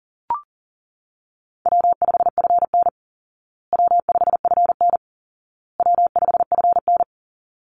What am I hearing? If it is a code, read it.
Morse code, heard as a keyed tone, about 40 wpm, 700 Hz. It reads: W5FN W5FN W5FN
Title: Easy Call Signs Repeated 3x - 40wpm